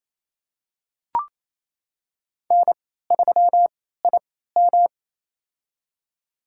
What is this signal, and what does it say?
Morse code, heard as a keyed tone, about 28 wpm, spaced as for 16 wpm, 700 Hz. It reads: N3IM
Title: USA Call Signs - 3x Character Spacing - 28wpm